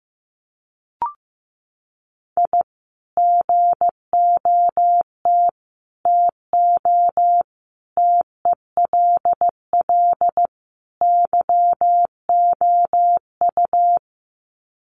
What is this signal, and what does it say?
Morse code, heard as a keyed tone, about 15 wpm, 700 Hz. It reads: I GOT TO TELL YOU
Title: Sets of 5 Words Encore2 15wpm